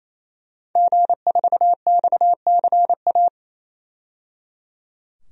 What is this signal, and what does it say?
Morse code, heard as a keyed tone, about 27 wpm, 700 Hz. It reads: G4XCA
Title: DXCC-100 Call Signs - 5 Characters - 28wpm